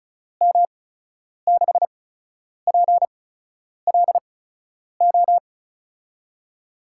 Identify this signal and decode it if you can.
Morse code, heard as a keyed tone, about 36 wpm, spaced as for 10 wpm, 700 Hz. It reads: M6PLO
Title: USA Call Signs - 6 Characters - 8x Character Spacing - 35wpm